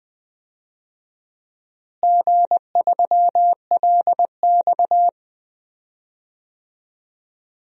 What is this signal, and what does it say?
Morse code, heard as a keyed tone, about 20 wpm, 700 Hz. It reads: G3LX
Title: DXCC-100 Call Signs - 4 Characters - Repeated 2x - 20wpm